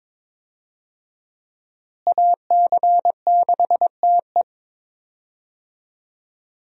Morse code, heard as a keyed tone, about 22 wpm, 700 Hz. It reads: AC6TE
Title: USA Call Signs - 5 Characters - Repeated 2x - 22wpm